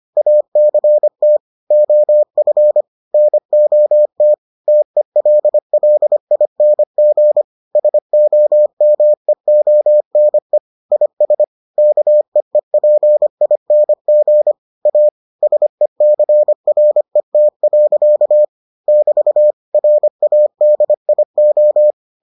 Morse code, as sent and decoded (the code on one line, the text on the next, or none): .- -.-. - / --- ..-. / -. --- - / - . .-.. .-.. .. -. --. / ... --- -- . --- -. / .. ... / -.- . .--. .. -. --. / .- / ... -.-. .-. - .-.-.- / -...- / .-. .- -.. .. ---
ACT OF NOT TELLING SOMEONE IS KEEPING A SECRET. <BT> RADIO